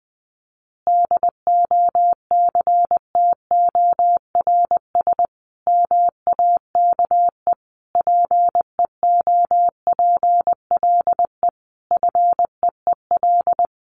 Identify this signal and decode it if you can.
Morse code, heard as a keyed tone, about 20 wpm, 700 Hz. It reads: DOCTORS MAKE PEOPLE FEEL
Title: Easy Phrases 20wpm